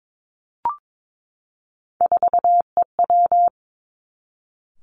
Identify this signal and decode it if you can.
Morse code, heard as a keyed tone, about 22 wpm, 700 Hz. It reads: 4EW
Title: Three Challenging Letter-Number Combinations 22wpm